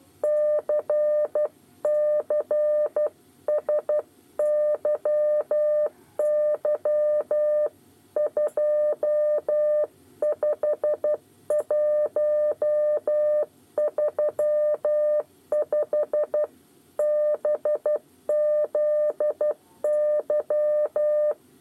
Morse code, heard as a keyed tone, about 12 wpm, 600 Hz. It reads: CCSYY 25135 BZY
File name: 8 bis lezione cw 2